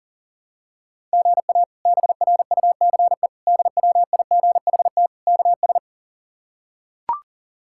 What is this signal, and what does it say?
Morse code, heard as a keyed tone, about 40 wpm, 700 Hz. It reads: GA BRUCE DWIGHT KS
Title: K1USN SST QSO Practice - 40wpm